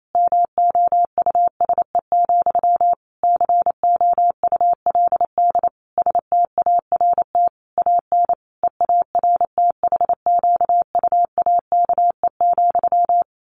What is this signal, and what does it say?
Morse code, heard as a keyed tone, about 28 wpm, 700 Hz. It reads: MOUSE, COULD START AN EARTHQUAKE,
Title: A Sound Of Thunder 28wpm